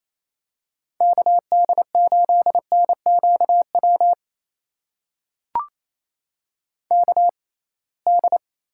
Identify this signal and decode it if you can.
Morse code, heard as a keyed tone, about 28 wpm, spaced as for 21 wpm, 700 Hz. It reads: KD8NQW KD
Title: USA Call Signs - 6 Characters - 6x Character Spacing - 28wpm